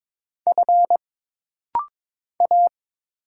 Morse code, heard as a keyed tone, about 22 wpm, 700 Hz. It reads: F A
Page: This rapid-fire timing is geared to push your ICR (Instant Character Recognition) skills.